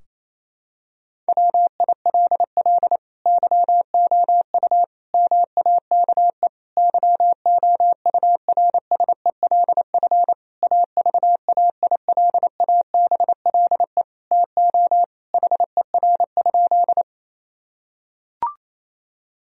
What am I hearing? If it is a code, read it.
Morse code, heard as a keyed tone, about 28 wpm, 700 Hz. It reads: WILL YOU MAKE YOURSELF AVAILABLE TO HER?
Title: Sentences from Top 1000 Words - Review Part 3 - 28wpm